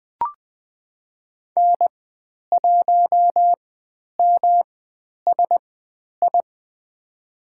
Morse code, heard as a keyed tone, about 20 wpm, spaced as for 10 wpm, 700 Hz. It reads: N1MSI
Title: 109 Lesson 18g - Letter m - Call Signs 20wpm